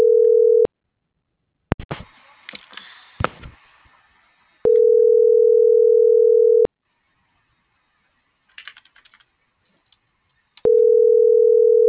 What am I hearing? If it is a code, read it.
Ringback tone (North America), 440 + 480 Hz.